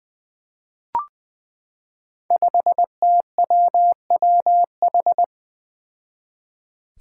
Morse code, heard as a keyed tone, about 20 wpm, 700 Hz. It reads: 5TWWH
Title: Five Challenging Letter-Number Combinations 20wpm